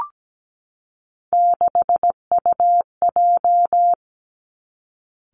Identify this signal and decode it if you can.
Morse code, heard as a keyed tone, about 17 wpm, 700 Hz. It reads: 6UJ
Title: Three Challenging Letter-Number Combinations 17wpm